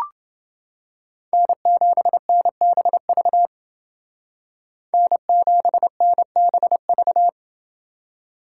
Morse code, heard as a keyed tone, about 29 wpm, 700 Hz. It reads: N7NBV N7NBV